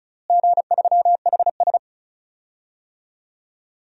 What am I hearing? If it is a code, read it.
Morse code, heard as a keyed tone, about 36 wpm, 700 Hz. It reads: G3HS
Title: DXCC-100 Call Signs - 4 Characters - Repeated 3x - 35wpm